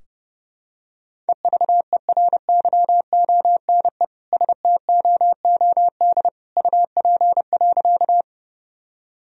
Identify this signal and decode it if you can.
Morse code, heard as a keyed tone, about 32 wpm, 700 Hz. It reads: EVERYONE STOOD UP.
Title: Sentences from Top 1000 Words - Review Part 2 - 30wpm